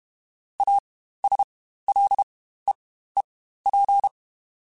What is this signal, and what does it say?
Morse code, heard as a keyed tone, about 32 wpm, spaced as for 15 wpm, 800 Hz. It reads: ASLEEP
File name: Quotes Spaced / Quotes Spaced-mp3s / ChildAsleep sp @30